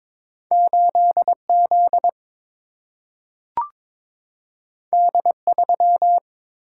Morse code, heard as a keyed tone, about 22 wpm, 700 Hz. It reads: 8Z D3
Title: Single-Letter Single-Number 22wpm